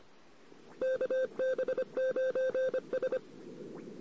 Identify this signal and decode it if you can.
Morse code, heard as a keyed tone, about 25 wpm, 1550 Hz, with noise. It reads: KB9S
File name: CW / 2008-ss-cw-k5tr-op-n5rz -14:00